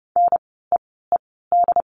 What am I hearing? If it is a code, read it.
Morse code, heard as a keyed tone, about 30 wpm, spaced as for 17 wpm, 700 Hz. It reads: NEED
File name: course / 30:10wpm / 237 Lesson 38f - Letter q - Words 30wpm